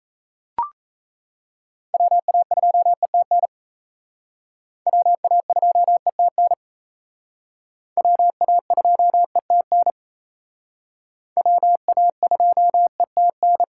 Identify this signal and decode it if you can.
Morse code, heard as a keyed tone, about 35 wpm, 700 Hz. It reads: WA2ETN WA2ETN WA2ETN WA2ETN
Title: Speed Racing - USA 2x3 Call Signs 28wpm